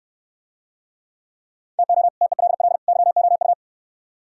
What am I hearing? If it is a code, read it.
Morse code, heard as a keyed tone, about 68 wpm, 700 Hz. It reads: TO TELL YOU